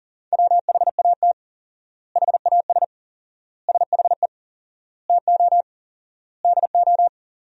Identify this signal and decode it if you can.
Morse code, heard as a keyed tone, about 40 wpm, 700 Hz. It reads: WHAT HAS SHE TO DO
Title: 4x Word-Spacing - Sentences from Top 100 Words - 40wpm